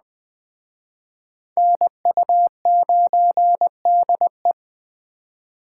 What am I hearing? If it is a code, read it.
Morse code, heard as a keyed tone, about 24 wpm, 700 Hz. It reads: NU9DE
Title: Speed Racing - USA 2x2 Call Signs 20wpm